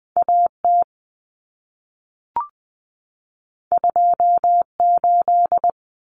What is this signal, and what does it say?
Morse code, heard as a keyed tone, about 20 wpm, 700 Hz. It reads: AT 28
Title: Common Cut Numbers - 2-Digit 20wpm